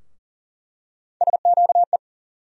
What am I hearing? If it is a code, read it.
Morse code, heard as a keyed tone, about 40 wpm, 700 Hz. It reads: SQE
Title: USA Call Sign Postfixes - 3 Letters 40wpm